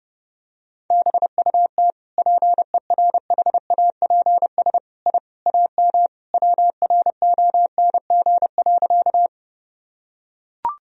Morse code, heard as a keyed tone, about 30 wpm, 700 Hz. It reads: BUT PERHAPS I AM WRONG.
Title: Sentences from Top 1000 Words - Review Part 2 - 30wpm